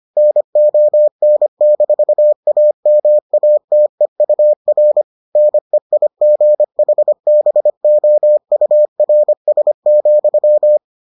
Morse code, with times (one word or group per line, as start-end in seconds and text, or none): 0.2-5.0 NON-AMATEUR
5.4-10.8 NEIGHBOURS,